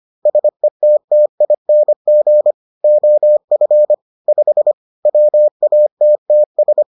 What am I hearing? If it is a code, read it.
Morse code, heard as a keyed tone, about 25 wpm, 600 Hz. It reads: SETTING OF 5 WATTS